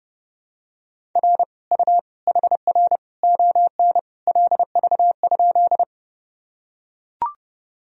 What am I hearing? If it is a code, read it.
Morse code, heard as a keyed tone, about 30 wpm, 700 Hz. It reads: R U HR ON LV?
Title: Sentences from Top 400 Words and Abbreviations 30wpm